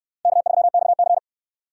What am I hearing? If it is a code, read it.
Morse code, heard as a keyed tone, about 62 wpm, 700 Hz. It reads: K3XC